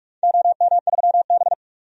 Morse code, heard as a keyed tone, about 42 wpm, 700 Hz. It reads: OM3B
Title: DXCC-100 Call Signs - 4 Characters - 45wpm